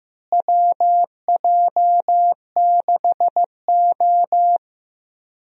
Morse code, heard as a keyed tone, about 15 wpm, 700 Hz. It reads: WJ6O
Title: USA Call Signs - 4 Characters - Repeated 2x - 15wpm